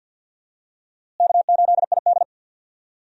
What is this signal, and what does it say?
Morse code, heard as a keyed tone, about 51 wpm, 700 Hz. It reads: K8ID
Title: USA Call Signs - 4 Characters - Repeated 2x - 50wpm